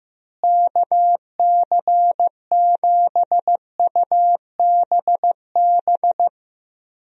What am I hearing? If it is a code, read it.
Morse code, heard as a keyed tone, about 16 wpm, 700 Hz. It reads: KC7UBB
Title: USA Call Signs - 6 Characters - Repeated 2x - 15wpm